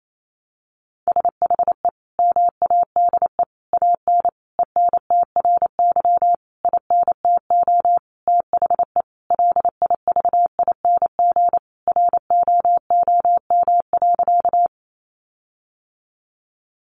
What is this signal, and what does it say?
Morse code, heard as a keyed tone, about 28 wpm, 700 Hz. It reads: SHE MADE AN ENTRY INTO THE LIVING ROOM.
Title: Sentences from Top 1500 Words 28wpm